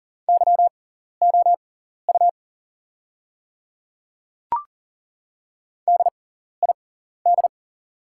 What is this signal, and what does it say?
Morse code, heard as a keyed tone, about 40 wpm, spaced as for 14 wpm, 690 Hz. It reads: YOU DID
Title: 6x Character Spacing - Top 200 Words - 40wpm